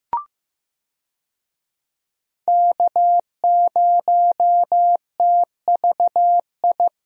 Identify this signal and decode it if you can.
Morse code, heard as a keyed tone, about 15 wpm, 700 Hz. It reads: K0TVI
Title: USA Call Signs - 5 Characters - Repeated 3x - 15wpm